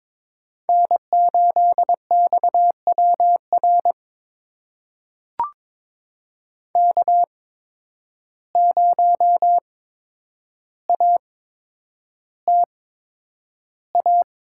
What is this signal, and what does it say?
Morse code, heard as a keyed tone, about 22 wpm, spaced as for 9 wpm, 700 Hz. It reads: N8XWR K0ATA
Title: USA Call Signs - 5 Characters - 8x Character Spacing - 22wpm